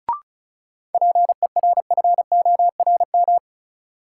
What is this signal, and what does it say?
Morse code, heard as a keyed tone, about 36 wpm, 700 Hz. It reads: PERFORM
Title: Sentences from Top 1000 Words - Review Part 5 - New Word Drill - 35wpm